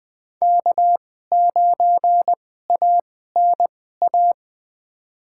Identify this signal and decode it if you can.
Morse code, heard as a keyed tone, about 20 wpm, spaced as for 14 wpm, 700 Hz. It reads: K9ANA
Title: USA Call Signs - 5 Characters - 2x Character Spacing - 20wpm